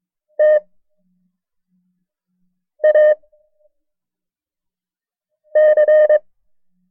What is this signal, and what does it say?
Morse code, heard as a keyed tone, about 22 wpm, 610 Hz. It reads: T A C
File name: FreeFiles / CW-practice / KB6MT-tape1-side2